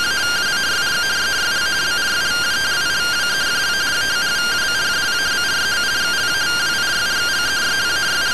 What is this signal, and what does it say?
Two-tone teleprinter signal, about 170 Hz shift.